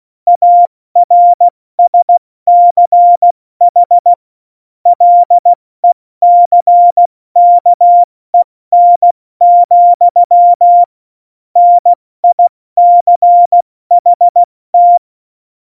Morse code, heard as a keyed tone, about 16 wpm, 700 Hz. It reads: ARSCH LECKEN, NICHT
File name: Morsekurs / Witze / Witz 23. April 2019, 70 F80